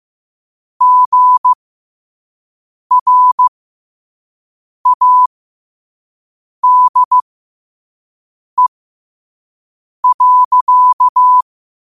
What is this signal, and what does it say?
Morse code, heard as a keyed tone, about 15 wpm, spaced as for 5 wpm, 1000 Hz. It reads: GRADE.
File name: qotdfiles / 5wpm